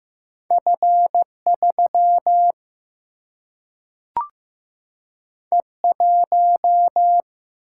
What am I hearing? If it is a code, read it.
Morse code, heard as a keyed tone, about 15 wpm, 700 Hz. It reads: F 3 E 1
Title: Single-Letter Single-Number 15wpm